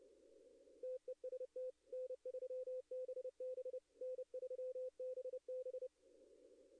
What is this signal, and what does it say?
Morse code, heard as a keyed tone, about 30 wpm, 510 Hz. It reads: TEST N3BB N3BB
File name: contests / arrl-160-k5tr-op-w5tn -2300